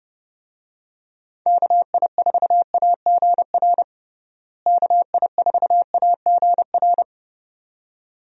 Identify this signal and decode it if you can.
Morse code, heard as a keyed tone, about 30 wpm, 700 Hz. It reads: KI4AGR KI4AGR
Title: USA Call Signs - 6 Characters - Repeated 2x - 30wpm